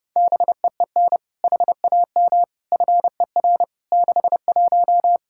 Morse code, heard as a keyed tone, about 30 wpm, 700 Hz. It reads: BEEN HAM FER 61
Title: K1USN SST QSO Practice 30wpm